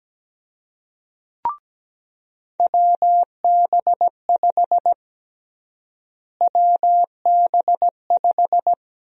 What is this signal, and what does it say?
Morse code, heard as a keyed tone, about 17 wpm, 700 Hz. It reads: WB5 WB5
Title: USA Call Sign Prefixes - Repeated 2x - 17wpm